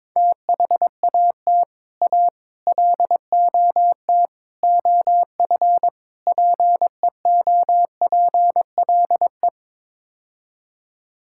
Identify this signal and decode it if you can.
Morse code, heard as a keyed tone, about 22 wpm, 700 Hz. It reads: THAT A LOT OF PEOPLE
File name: practice / 22wpm / Build A Phrase - 5-Word Phrases - 22wpm